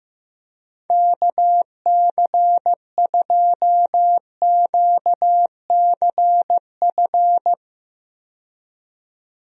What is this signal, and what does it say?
Morse code, heard as a keyed tone, about 15 wpm, 700 Hz. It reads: KC2QCF
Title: USA Call Signs - 6 Characters - 15wpm